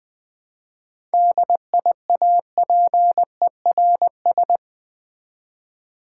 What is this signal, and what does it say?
Morse code, heard as a keyed tone, about 20 wpm, 700 Hz. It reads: DIAPERS
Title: Easy Words with Hints - 20wpm